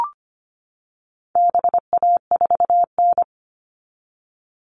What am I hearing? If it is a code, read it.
Morse code, heard as a keyed tone, about 25 wpm, 700 Hz. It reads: BA4N
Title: All Cut Numbers - 4-Digit 25wpm